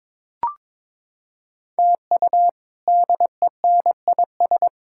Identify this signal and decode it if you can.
Morse code, heard as a keyed tone, about 22 wpm, 700 Hz. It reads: TU DENIS
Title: K1USN SST QSO Practice - 22wpm